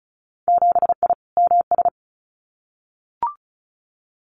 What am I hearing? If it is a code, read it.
Morse code, heard as a keyed tone, about 34 wpm, 700 Hz. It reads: 7I MS
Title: Winter Field Day Exchanges Encore 35wpm